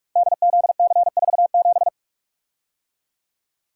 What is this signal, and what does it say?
Morse code, heard as a keyed tone, about 44 wpm, 700 Hz. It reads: DZC47
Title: Code Groups - 5 Letter:Numbers - 45wpm